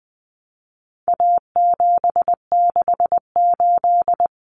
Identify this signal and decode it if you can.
Morse code, heard as a keyed tone, about 20 wpm, 700 Hz. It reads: A768
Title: Common Cut Numbers - 4-Digit 20wpm